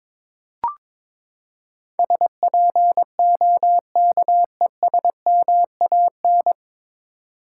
Morse code, heard as a keyed tone, about 22 wpm, 700 Hz. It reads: SPOKESMAN
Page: Morse Code Ninja practice 22wpm